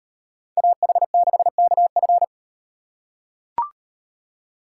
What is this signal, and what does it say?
Morse code, heard as a keyed tone, about 36 wpm, 700 Hz. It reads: AH6KF